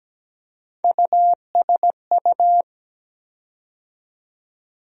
Morse code, heard as a keyed tone, about 17 wpm, 700 Hz. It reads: USU
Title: USA Call Sign Postfixes - 3 Letters 17wpm